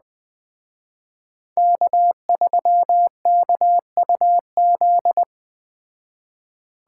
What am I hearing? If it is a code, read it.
Morse code, heard as a keyed tone, about 20 wpm, 700 Hz. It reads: K3KUZ
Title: Speed Racing - USA 1x3 Call Signs 17wpm